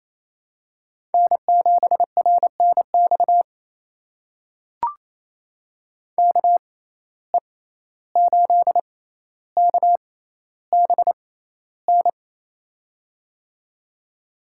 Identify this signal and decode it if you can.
Morse code, heard as a keyed tone, about 28 wpm, spaced as for 14 wpm, 700 Hz. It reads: N7RNX KE8KBN
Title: USA Call Signs - 6x Character Spacing - 28wpm